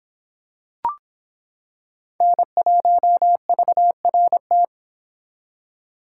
Morse code, heard as a keyed tone, about 26 wpm, 700 Hz. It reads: N1VRT